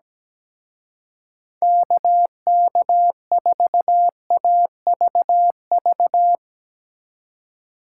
Morse code, heard as a keyed tone, about 18 wpm, 700 Hz. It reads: KK4AVV
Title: Speed Racing - USA 2x3 Call Signs 17wpm